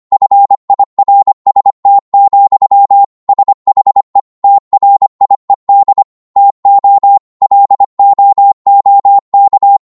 Morse code, heard as a keyed tone, about 25 wpm, 800 Hz. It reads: FIRST, SHE TRIED TO LOOK